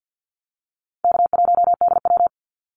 Morse code, heard as a keyed tone, about 50 wpm, 700 Hz. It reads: K1DW
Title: Easy Call Signs - 50wpm